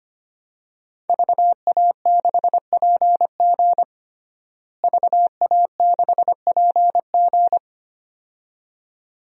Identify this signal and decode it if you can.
Morse code, heard as a keyed tone, about 25 wpm, 700 Hz. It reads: VA6PG VA6PG
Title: DXCC-100 Call Signs - 5 Characters - Repeated 2x - 25wpm